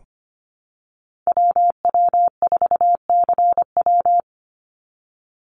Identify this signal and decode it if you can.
Morse code, heard as a keyed tone, about 25 wpm, 700 Hz. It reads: WW4CW
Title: USA Call Signs 25wpm